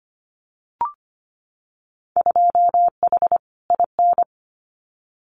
Morse code, heard as a keyed tone, about 25 wpm, 700 Hz. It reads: 2H IN
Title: Winter Field Day Exchanges Encore 25wpm